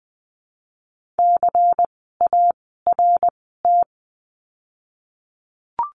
Morse code, heard as a keyed tone, about 20 wpm, 700 Hz. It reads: C A R T
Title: 094 Lesson 16f - Letter c - Words 20wpm